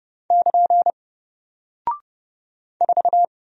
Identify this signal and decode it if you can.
Morse code, heard as a keyed tone, about 30 wpm, 700 Hz. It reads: <KN> 4
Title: Single Character (Letter:Number:Punctuation:Prosign) Alt 30wpm